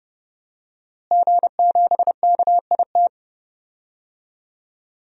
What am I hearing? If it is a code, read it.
Morse code, heard as a keyed tone, about 30 wpm, 700 Hz. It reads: G7KIT
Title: DXCC-100 Call Signs - 5 Characters - Repeated 2x - 30wpm